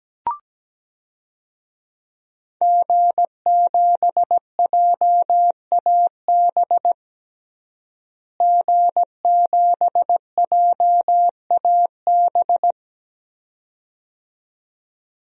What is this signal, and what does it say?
Morse code, heard as a keyed tone, about 17 wpm, 700 Hz. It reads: G7JAB G7JAB
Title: DXCC-100 Call Signs - 5 Characters - Repeated 2x - 17wpm